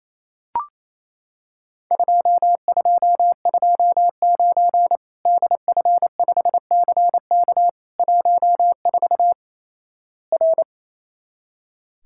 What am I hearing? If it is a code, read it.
Morse code, heard as a keyed tone, about 28 wpm, 700 Hz. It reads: 2229 DF5CK 14 R
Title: Worked All Europe DX Contest - QTC Exchange - 28wpm